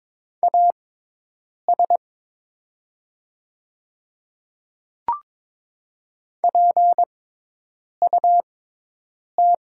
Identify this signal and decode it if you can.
Morse code, heard as a keyed tone, about 23 wpm, spaced as for 8 wpm, 690 Hz. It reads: AS PUT
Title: 6x Character Spacing - Top 200 Words - 22wpm